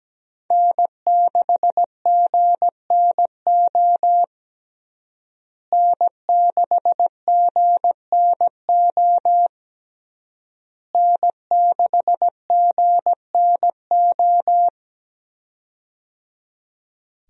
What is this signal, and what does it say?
Morse code, heard as a keyed tone, about 17 wpm, 700 Hz. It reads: N6GNO N6GNO N6GNO
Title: USA Call Signs - 5 Characters - Repeated 3x - 17wpm